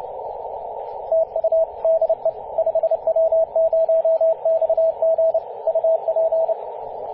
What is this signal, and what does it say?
Morse code, heard as a keyed tone, about 30 wpm, 650 Hz, with radio noise. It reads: TU DE 5W0XG UP
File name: 5w0xg-18-075mhz-cw